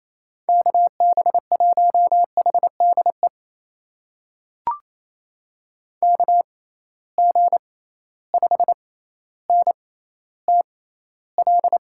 Morse code, heard as a keyed tone, about 28 wpm, spaced as for 14 wpm, 700 Hz. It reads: KB1HDE KG5NTL
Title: USA Call Signs - 6 Characters - 6x Character Spacing - 28wpm